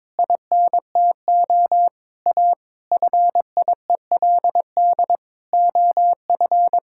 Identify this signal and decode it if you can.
Morse code, heard as a keyed tone, about 21 wpm, 700 Hz. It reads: INTO A FIELD OF
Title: Pangram Challenge v2 - 22wpm